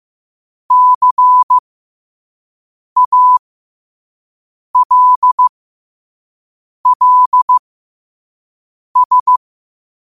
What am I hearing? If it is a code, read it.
Morse code, heard as a keyed tone, about 15 wpm, spaced as for 5 wpm, 1000 Hz. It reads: CALLS
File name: qotdfiles / 5wpm